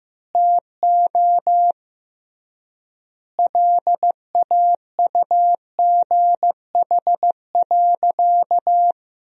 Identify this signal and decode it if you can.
Morse code, heard as a keyed tone, about 15 wpm, 700 Hz. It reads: TO LAUGH.
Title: 3x Word-Spacing - Sentences from Top 500 Words - 15wpm